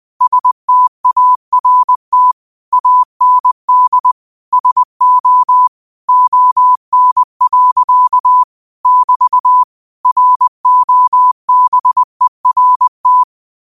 Morse code, heard as a keyed tone, about 20 wpm, 1000 Hz. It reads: START AND SO ON. <BT> ROBERT